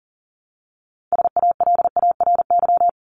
Morse code, heard as a keyed tone, about 40 wpm, 700 Hz. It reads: SALARY
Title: Top 1751-2000 Words 40wpm